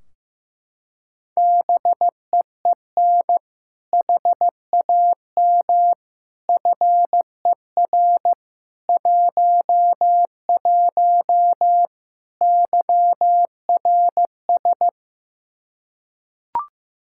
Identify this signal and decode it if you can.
Morse code, heard as a keyed tone, about 15 wpm, 700 Hz. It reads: BEEN HAM FER 11 YRS
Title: Practice QSO - Exchange 3 - More Info - 15wpm